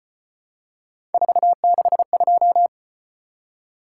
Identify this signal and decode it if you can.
Morse code, heard as a keyed tone, about 34 wpm, 700 Hz. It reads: 462